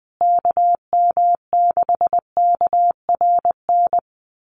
Morse code, heard as a keyed tone, about 20 wpm, 700 Hz. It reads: KM6KRN
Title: USA Call Signs Repeated 2x 20wpm